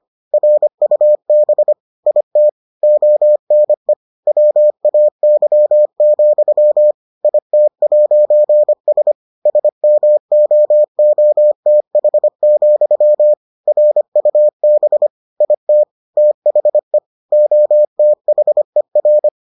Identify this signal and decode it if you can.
Morse code, heard as a keyed tone, about 25 wpm, 600 Hz. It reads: RUB IT ONE WAY, IT'S SMOOTH, RUB IT THE OTHER